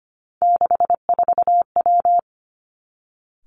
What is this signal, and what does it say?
Morse code, heard as a keyed tone, about 25 wpm, 700 Hz. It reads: 64W
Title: Three Challenging Letter-Number Combinations 25wpm